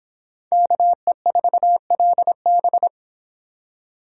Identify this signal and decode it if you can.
Morse code, heard as a keyed tone, about 25 wpm, 700 Hz. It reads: KE4LB